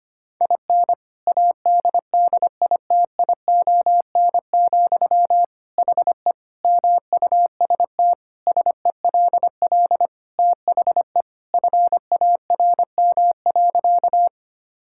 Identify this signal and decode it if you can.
Morse code, heard as a keyed tone, about 25 wpm, 700 Hz. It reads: IN ADDITION, HE MUST SELL THE FARM.
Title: Sentences from Top 1000 Words - Review Part 6 - 25wpm